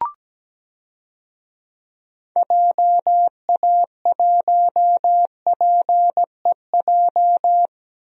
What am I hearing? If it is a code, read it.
Morse code, heard as a keyed tone, about 17 wpm, 700 Hz. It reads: JA1PEJ
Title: DXCC-100 Call Signs - 6 Characters - Repeated 3x - 17wpm